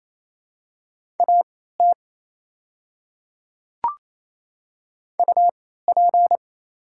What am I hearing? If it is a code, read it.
Morse code, heard as a keyed tone, about 28 wpm, spaced as for 16 wpm, 700 Hz. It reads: AT UP
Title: 3x Character Spacing - Top 100 Words - 28wpm